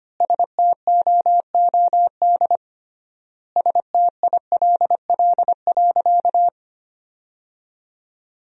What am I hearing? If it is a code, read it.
Morse code, heard as a keyed tone, about 25 wpm, 700 Hz. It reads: STOOD STILL.
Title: 3x Word-Spacing - Sentences from Top 500 Words - 25wpm